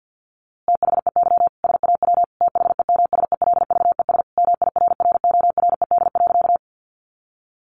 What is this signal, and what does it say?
Morse code, heard as a keyed tone, about 50 wpm, 700 Hz. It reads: THEY SAW THEMSELVES MIRRORED.
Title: A Sound Of Thunder 50wpm